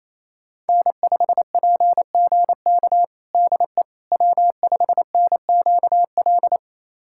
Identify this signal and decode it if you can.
Morse code, heard as a keyed tone, about 27 wpm, 700 Hz. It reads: N5PGK DE W5NQL
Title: USA Callsign DE Callsign - 5 Characters - 28wpm